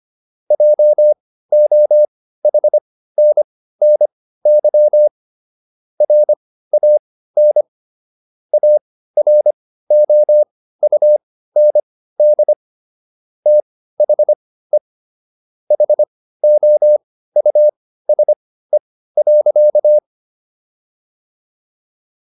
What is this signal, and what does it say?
Morse code, heard as a keyed tone, about 25 wpm, spaced as for 15 wpm, 600 Hz. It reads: JOHNNY RAN AROUND THE HOUSE.